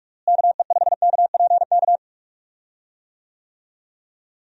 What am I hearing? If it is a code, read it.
Morse code, heard as a keyed tone, about 41 wpm, 700 Hz. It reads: KE5KPK
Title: USA Call Signs - 6 Characters - Repeated 2x - 45wpm